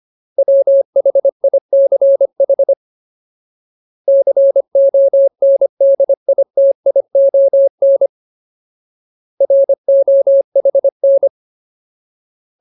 Morse code, heard as a keyed tone, about 25 wpm, 550 Hz. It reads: WHICH CONDITION ROHN
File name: Ultimate Word List / 25WPM Ultimate